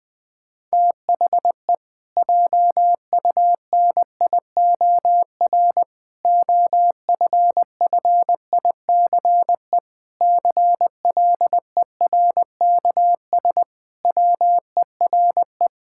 Morse code, heard as a keyed tone, about 21 wpm, 700 Hz. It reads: THE JUNIOR OFFICE CLERKS WERE
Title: Pangram Challenge v2 - 20wpm